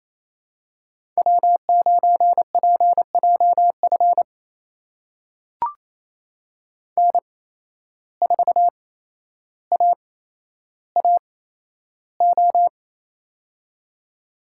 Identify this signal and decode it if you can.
Morse code, heard as a keyed tone, about 28 wpm, spaced as for 12 wpm, 700 Hz. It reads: W9PJF N4AAO
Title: USA Call Signs - 5 Characters - 8x Character Spacing - 28wpm